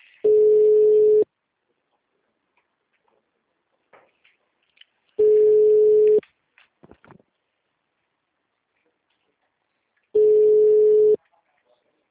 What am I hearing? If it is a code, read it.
Ringback tone (Europe, 425 Hz).
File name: out- -2004- - - .30167